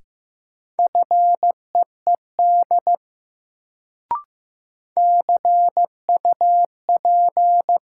Morse code, heard as a keyed tone, about 15 wpm, 700 Hz. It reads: FEED CUP
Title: Sentences from Top 1600 Words - New Word Drill Alt - 15wpm